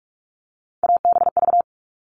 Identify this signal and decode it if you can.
Morse code, heard as a keyed tone, about 44 wpm, 700 Hz. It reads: ABV
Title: All Cut Numbers - 3-Digit 45wpm